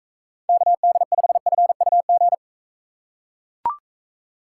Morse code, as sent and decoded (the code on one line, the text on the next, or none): -.- -.. ..... ..-. ..- --.
KD5FUG